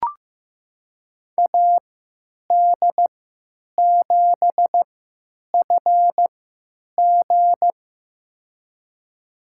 Morse code, heard as a keyed tone, about 15 wpm, spaced as for 9 wpm, 700 Hz. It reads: AD7FG
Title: USA Call Signs - 5 Characters - 3x Character Spacing - 15wpm